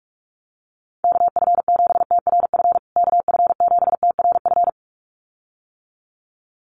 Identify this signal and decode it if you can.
Morse code, heard as a keyed tone, about 44 wpm, 700 Hz. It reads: KF7TRF KF7TRF
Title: USA Call Signs Repeated 2x 45wpm